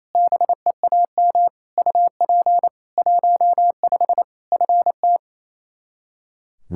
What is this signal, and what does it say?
Morse code, heard as a keyed tone, about 28 wpm, 700 Hz. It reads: BEAM UP 15 FT
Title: Practice QSO - Exchange 2 - Info - 28wpm